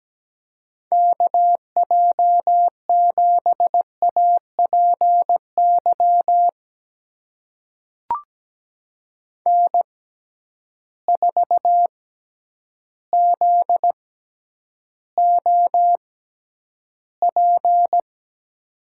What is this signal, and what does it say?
Morse code, heard as a keyed tone, about 17 wpm, spaced as for 9 wpm, 700 Hz. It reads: KJ7APY N4ZOP
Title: USA Call Signs - 6x Character Spacing - 17wpm